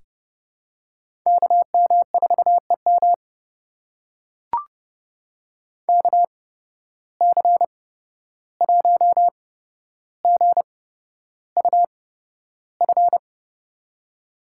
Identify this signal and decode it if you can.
Morse code, heard as a keyed tone, about 31 wpm, spaced as for 12 wpm, 700 Hz. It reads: KM4EM KC1GUF
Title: USA Call Signs - 8x Character Spacing - 30wpm